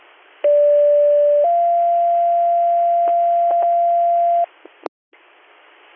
Two-tone page, heard as a one-second tone, then a three-second tone, about 580 then 690 Hz. It is heard from 0.4 to 4.5 s.